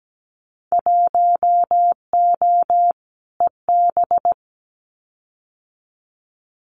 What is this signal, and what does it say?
Morse code, heard as a keyed tone, about 17 wpm, 700 Hz. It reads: 1O EB
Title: Winter Field Day Exchanges 17wpm